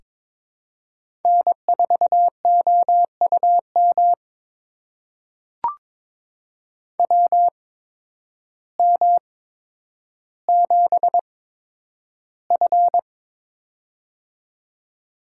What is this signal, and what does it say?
Morse code, heard as a keyed tone, about 22 wpm, spaced as for 10 wpm, 700 Hz. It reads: N4OUM WM7F
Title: USA Call Signs - 8x Character Spacing - 22wpm